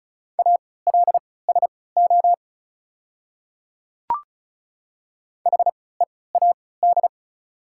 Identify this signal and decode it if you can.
Morse code, heard as a keyed tone, about 34 wpm, spaced as for 20 wpm, 700 Hz. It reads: ALSO HEAD
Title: 3x Character Spacing - Top 300 Words - 35wpm